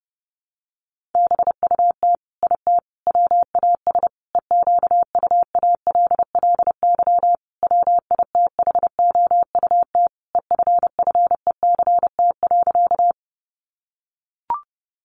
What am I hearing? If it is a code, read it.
Morse code, heard as a keyed tone, about 30 wpm, 700 Hz. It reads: BUT IT WAS EQUALLY WITHOUT EFFECT.
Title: Sentences from Top 1400 Words 30wpm